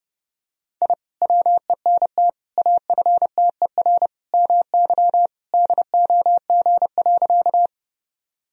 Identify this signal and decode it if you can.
Morse code, heard as a keyed tone, about 32 wpm, 700 Hz. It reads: I WENT AFTER MY DOG.
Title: Sentences from Top 1000 Words - Review Part 2 - 30wpm